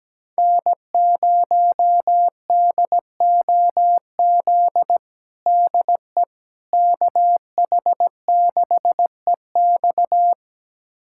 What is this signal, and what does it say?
Morse code, heard as a keyed tone, about 17 wpm, 700 Hz. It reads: N0DOZ DE KH6EX
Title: USA Callsign DE Callsign - 5 Characters - 17wpm